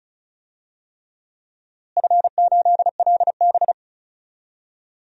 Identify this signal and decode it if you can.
Morse code, heard as a keyed tone, about 36 wpm, 700 Hz. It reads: F8LB
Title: DXCC-100 Call Signs - 4 Characters - Repeated 3x - 35wpm